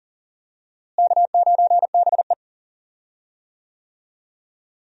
Morse code, heard as a keyed tone, about 40 wpm, 700 Hz. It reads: K9BE
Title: USA Call Signs - 4 Characters - Repeated 3x - 40wpm